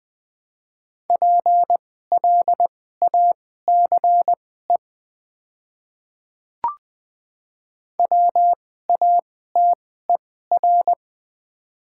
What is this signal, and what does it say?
Morse code, heard as a keyed tone, about 20 wpm, spaced as for 14 wpm, 700 Hz. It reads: PLACE WATER
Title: 2x Character Spacing - Top 200 Words - 20wpm